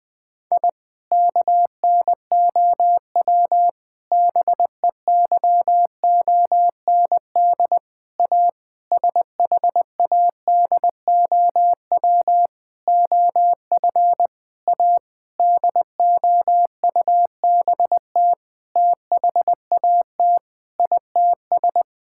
Morse code, heard as a keyed tone, about 20 wpm, 700 Hz. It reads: I KNOW BEYOND A SHADOW OF A DOUBT THAT ITS
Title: Sentences with 7-Word Phrases - 20wpm